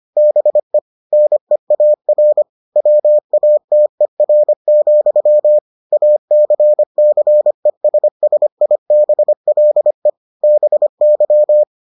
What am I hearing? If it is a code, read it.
Morse code, heard as a keyed tone, about 25 wpm, 600 Hz. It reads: BE NEAR WATER, ACCESSIBLE BY